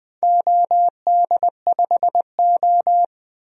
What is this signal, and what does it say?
Morse code, heard as a keyed tone, about 20 wpm, 700 Hz. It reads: OD5O
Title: DXCC-100 Call Signs - 4 Characters - 20wpm